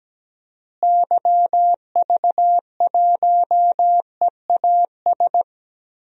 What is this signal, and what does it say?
Morse code, heard as a keyed tone, about 17 wpm, 700 Hz. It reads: YV1EAS
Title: DXCC-100 Call Signs - 6 Characters - Repeated 2x - 17wpm